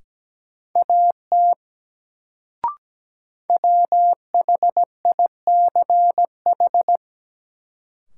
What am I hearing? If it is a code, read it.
Morse code, heard as a keyed tone, about 17 wpm, 700 Hz. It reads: AT WHICH
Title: Sentences from Top 100 Words - New Word Drill - 17wpm